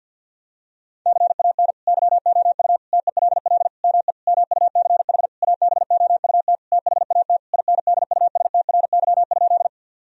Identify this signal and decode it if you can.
Morse code, heard as a keyed tone, about 50 wpm, 700 Hz. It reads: CAN YOU TELL ME MUCH ABOUT THAT INDUSTRY?
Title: Sentences from Top 1000 Words - Review Part 4 - 50wpm